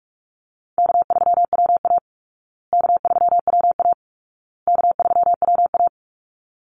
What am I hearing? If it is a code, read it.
Morse code, heard as a keyed tone, about 44 wpm, 700 Hz. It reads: K3WA K3WA K3WA
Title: Easy Call Signs Repeated 3x - 45wpm